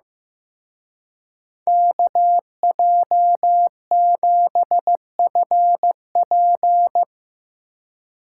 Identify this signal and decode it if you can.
Morse code, heard as a keyed tone, about 17 wpm, 700 Hz. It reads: KJ7FP
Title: Speed Racing - USA 2x2 Call Signs 15wpm